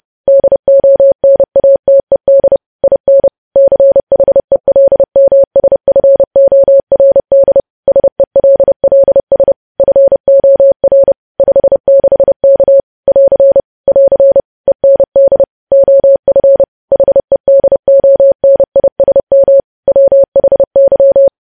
Morse code, heard as a keyed tone, about 30 wpm, 550 Hz. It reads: DONATED IN CHELMSFORD SELLS FOR 56K <AR> <AR> END OF HEDONISM WHY
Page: news headlines in morse code at 30 wpm